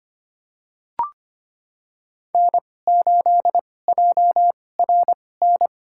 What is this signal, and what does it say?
Morse code, heard as a keyed tone, about 25 wpm, spaced as for 18 wpm, 700 Hz. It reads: N8JRN
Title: USA Call Signs - 5 Characters - 2x Character Spacing - 25wpm